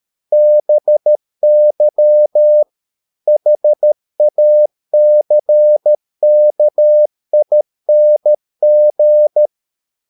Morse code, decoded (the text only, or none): BY HACKING